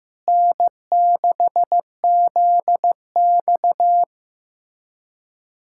Morse code, heard as a keyed tone, about 15 wpm, 700 Hz. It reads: N6ZX
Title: USA Callsign DE Callsign - 4 Characters - 15wpm